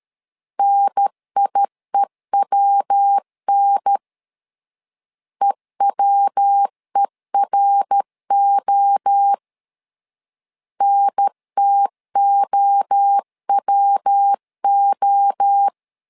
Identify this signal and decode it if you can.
Morse code, heard as a keyed tone, about 13 wpm, 800 Hz. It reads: NIEWN EWERO NTOWO
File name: LEZIONI / 3 MWRO / 3 MWRO 12wpm 1730